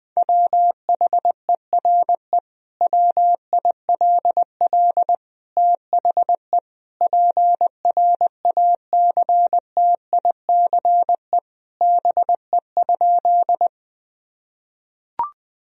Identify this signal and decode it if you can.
Morse code, heard as a keyed tone, about 20 wpm, 700 Hz. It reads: WHERE WILL THE PRACTICE BE?
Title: Sentences from Top 1000 Words - Review Part 4 - 20wpm